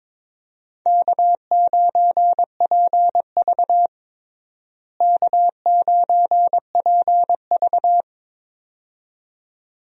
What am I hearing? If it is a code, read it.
Morse code, heard as a keyed tone, about 21 wpm, 700 Hz. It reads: K9PV K9PV
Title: USA Call Signs - 4 Characters - Repeated 2x - 22wpm